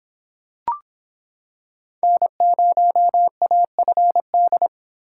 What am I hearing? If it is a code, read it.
Morse code, heard as a keyed tone, about 26 wpm, 700 Hz. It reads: N0AFD